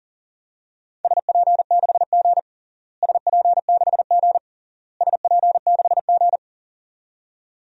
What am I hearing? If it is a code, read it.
Morse code, heard as a keyed tone, about 40 wpm, 700 Hz. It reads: SP6G SP6G SP6G
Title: DXCC-100 Call Signs - 4 Characters - Repeated 3x - 40wpm